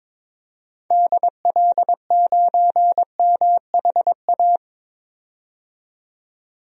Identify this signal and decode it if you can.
Morse code, heard as a keyed tone, about 22 wpm, 700 Hz. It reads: DL9MHA
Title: DXCC-100 Call Signs - 6 Characters - 22wpm